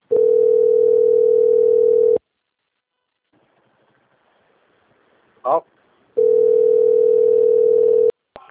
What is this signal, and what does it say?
Ringback tone (North America), 440 + 480 Hz, with speech.
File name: out- - - - - .34770